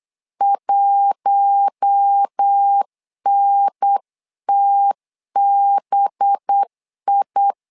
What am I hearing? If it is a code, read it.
Morse code, heard as a keyed tone, about 8 wpm, 800 Hz. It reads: 1NTBI
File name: LEZIONI / 5 LBVC / 5 LB 8wpm 2816